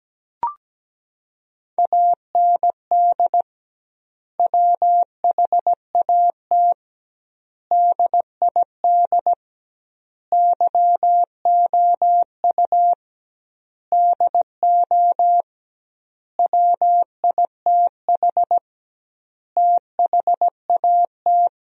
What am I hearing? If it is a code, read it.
Morse code, heard as a keyed tone, about 17 wpm, 700 Hz. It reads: AND WHAT DID YOU DO WITH THAT
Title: 2x Word-Spacing - Sentences from Top 300 Words - 17wpm